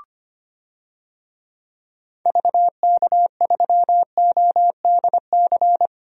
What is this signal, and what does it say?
Morse code, heard as a keyed tone, about 25 wpm, 700 Hz. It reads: VK3ODC
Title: DXCC-100 Call Signs - 6 Characters - 25wpm